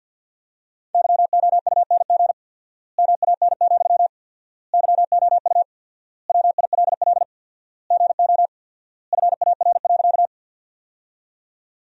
Morse code, heard as a keyed tone, about 50 wpm, 700 Hz. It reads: YOUNG MAN, YOU WILL GO FAR.
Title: 4x Word-Spacing - Sentences from Top 400 Words - 50wpm